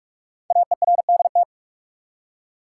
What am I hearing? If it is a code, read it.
Morse code, heard as a keyed tone, about 44 wpm, 690 Hz. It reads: AERDT
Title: Code Groups - 5 Letters - 45wpm